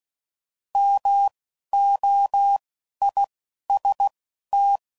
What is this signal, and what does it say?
Morse code, heard as a keyed tone, about 16 wpm, spaced as for 11 wpm, 800 Hz. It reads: MOIST